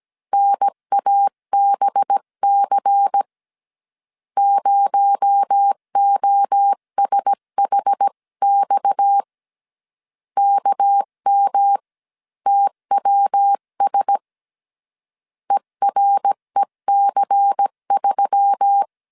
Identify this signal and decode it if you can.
Morse code, heard as a keyed tone, about 17 wpm, 800 Hz. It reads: NABC 0OSHX KMTWS EREC3
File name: CW / DET / 1 6 1500C S C 16wpm 2733